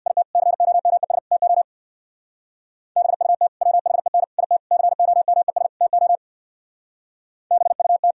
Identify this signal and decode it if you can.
Morse code, heard as a keyed tone, about 60 wpm, 690 Hz. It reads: IT COMES TO BUT WHEN IT COMES TO BUT
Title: Speed-Racing - Sets of 5 Words 50wpm